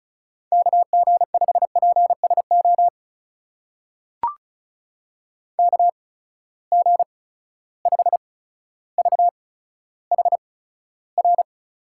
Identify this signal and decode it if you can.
Morse code, heard as a keyed tone, about 34 wpm, spaced as for 15 wpm, 700 Hz. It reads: KG5PSO KG5VHR
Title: USA Call Signs - 6 Characters - 8x Character Spacing - 35wpm